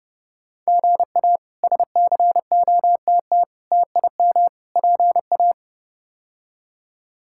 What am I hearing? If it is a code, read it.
Morse code, heard as a keyed tone, about 31 wpm, 700 Hz. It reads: GA SCOTT TIM PA
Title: K1USN SST QSO Practice - 30wpm